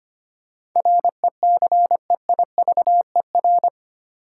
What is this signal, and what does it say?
Morse code, heard as a keyed tone, about 25 wpm, 700 Hz. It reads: RECEIVER
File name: Eight Letter Words - 25wpm